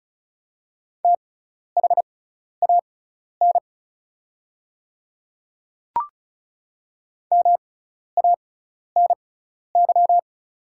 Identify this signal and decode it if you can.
Morse code, heard as a keyed tone, about 35 wpm, spaced as for 12 wpm, 690 Hz. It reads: THAN MANY
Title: 6x Character Spacing - Top 200 Words - 35wpm